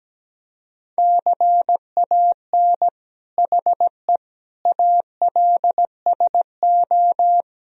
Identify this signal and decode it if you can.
Morse code, heard as a keyed tone, about 17 wpm, 700 Hz. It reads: CAN HE ALSO
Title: Sentences from Top 1000 Words - Review Part 3 - 17wpm